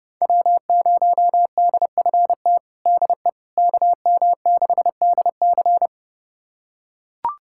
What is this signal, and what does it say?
Morse code, heard as a keyed tone, about 30 wpm, 700 Hz. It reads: W0DFT DE KM6DC
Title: USA Callsign DE Callsign - 5 Characters - 30wpm